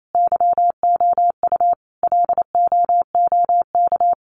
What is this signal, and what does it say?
Morse code, heard as a keyed tone, about 28 wpm, 700 Hz. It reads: YOU LOOK
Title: Sentences from Top 100 Words 28wpm